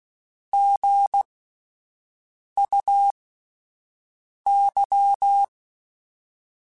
Morse code, heard as a keyed tone, about 16 wpm, spaced as for 5 wpm, 800 Hz. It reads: GUY
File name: Spaced / Chuckles Spaced-mp3s / HumanErr sp @10